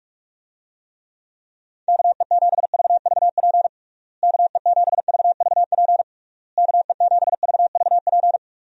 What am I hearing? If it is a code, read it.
Morse code, heard as a keyed tone, about 46 wpm, 700 Hz. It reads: KE7VVP KE7VVP KE7VVP
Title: USA Call Signs - 6 Characters - Repeated 3x - 45wpm